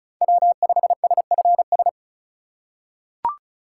Morse code, heard as a keyed tone, about 34 wpm, 700 Hz. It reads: W5SFS
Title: USA Call Signs - 5 Characters - 4x Character Spacing - 35wpm